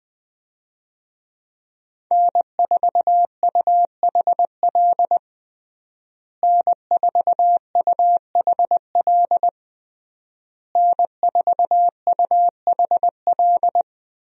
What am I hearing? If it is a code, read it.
Morse code, heard as a keyed tone, about 20 wpm, 700 Hz. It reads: N4UHL N4UHL N4UHL
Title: USA Call Signs - 5 Characters - Repeated 3x - 20wpm